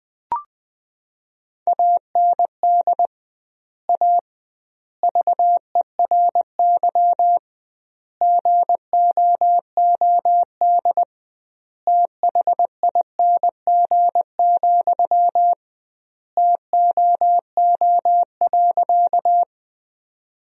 Morse code, heard as a keyed tone, about 20 wpm, 700 Hz. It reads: AND A VERY GOOD THING, TOO.
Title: 2x Word-Spacing - Sentences from Top 200 Words - 20wpm